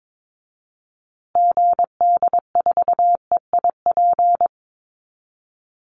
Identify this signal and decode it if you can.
Morse code, heard as a keyed tone, about 22 wpm, 690 Hz. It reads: GD4EIP
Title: Call Signs Repeated 3x - 22wpm